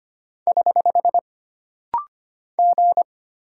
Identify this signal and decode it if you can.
Morse code, heard as a keyed tone, about 25 wpm, 700 Hz. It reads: <HH> G
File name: Single Character (Letter:Number:Punctuation:Prosign) - Rapid-Fire - 25wpm